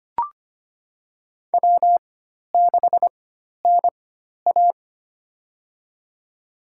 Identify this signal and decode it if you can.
Morse code, heard as a keyed tone, about 25 wpm, spaced as for 12 wpm, 700 Hz. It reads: W6NA
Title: USA Call Signs - 4 Characters - 4x Character Spacing - 25wpm